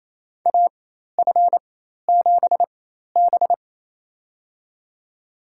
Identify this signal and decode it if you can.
Morse code, heard as a keyed tone, about 27 wpm, spaced as for 13 wpm, 700 Hz. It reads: AF7B
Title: USA Call Signs - 4x Character Spacing - 28wpm